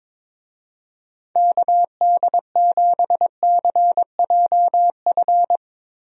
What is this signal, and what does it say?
Morse code, heard as a keyed tone, about 22 wpm, 700 Hz. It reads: KD7CJF
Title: USA Call Signs - 6 Characters - Repeated 3x - 22wpm